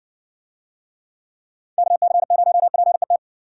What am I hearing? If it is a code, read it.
Morse code, heard as a keyed tone, about 59 wpm, 700 Hz. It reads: KC0PET